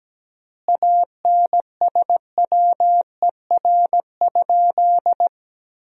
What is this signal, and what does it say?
Morse code, heard as a keyed tone, about 17 wpm, 700 Hz. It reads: ANSWER?
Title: 4x Word-Spacing - Sentences from Top 300 Words - 17wpm